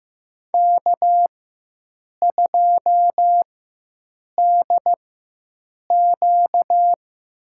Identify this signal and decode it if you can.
Morse code, heard as a keyed tone, about 15 wpm, spaced as for 7 wpm, 700 Hz. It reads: K2DQ
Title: USA Call Signs - 4x Character Spacing - 15wpm